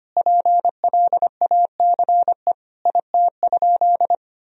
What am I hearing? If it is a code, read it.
Morse code, heard as a keyed tone, about 25 wpm, 700 Hz. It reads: PLACE IT?
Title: Sentences from Top 200 Words and Abbreviations 25wpm